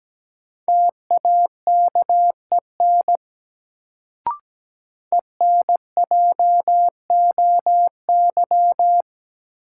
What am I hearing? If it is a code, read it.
Morse code, heard as a keyed tone, about 17 wpm, 700 Hz. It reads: TAKEN ENJOY
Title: Sentences from Top 1000 Words - Review Part 4 - New Word Drill - 17wpm